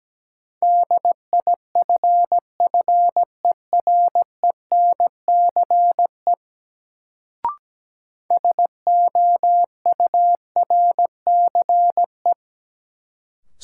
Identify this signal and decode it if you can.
Morse code, heard as a keyed tone, about 17 wpm, 700 Hz. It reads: DIFFERENCE SOURCE
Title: Sentences from Top 800 Words - New Word Drill - 17wpm